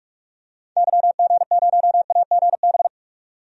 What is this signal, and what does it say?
Morse code, heard as a keyed tone, about 42 wpm, 700 Hz. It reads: YG0AGB
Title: DXCC-100 Call Signs - 6 Characters - 45wpm